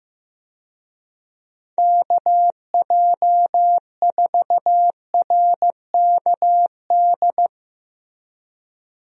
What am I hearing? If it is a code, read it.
Morse code, heard as a keyed tone, about 15 wpm, 700 Hz. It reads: KJ4RKD
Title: USA Call Signs - 6 Characters - Repeated 3x - 15wpm